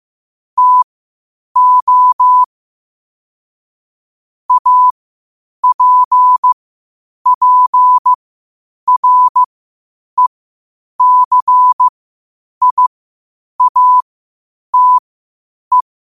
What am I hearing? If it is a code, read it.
Morse code, heard as a keyed tone, about 15 wpm, spaced as for 8 wpm, 1000 Hz. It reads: TO APPRECIATE
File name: qotdfiles / 8wpm